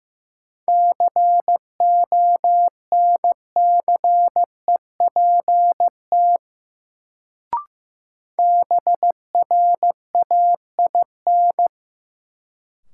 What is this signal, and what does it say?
Morse code, heard as a keyed tone, about 16 wpm, 700 Hz. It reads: CONCEPT BRAIN
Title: Sentences from Top 1100 Words - New Word Drill Alt - 15wpm